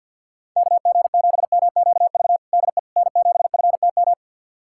Morse code, heard as a keyed tone, about 50 wpm, 700 Hz. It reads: KG7MQV DE N7FTM
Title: USA Callsign DE Callsign 50wpm